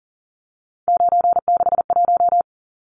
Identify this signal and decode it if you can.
Morse code, heard as a keyed tone, about 40 wpm, 700 Hz. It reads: 961
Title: Numbers - 3-Digit 40wpm